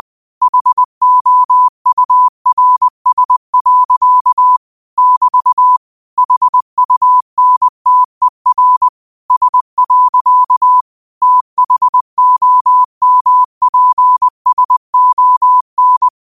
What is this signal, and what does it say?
Morse code, heard as a keyed tone, about 20 wpm, 1000 Hz. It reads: HOURS. <BT> HUNTER S. THOMPSON
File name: qotdfiles / 20wpm